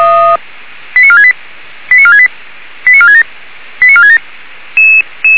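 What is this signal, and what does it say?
Five-tone selective call (ZVEI), heard as quick runs of five tones, tones of about 80 ms. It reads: 89468 89468 89467 89467
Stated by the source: ZVEI five-tone selective call, received by SDR, archived by feuerwehr-orlamuende.org (fire brigade).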